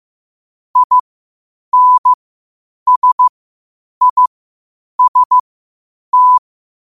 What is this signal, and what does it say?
Morse code, heard as a keyed tone, about 15 wpm, spaced as for 8 wpm, 1000 Hz. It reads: INSIST
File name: qotdfiles / 8wpm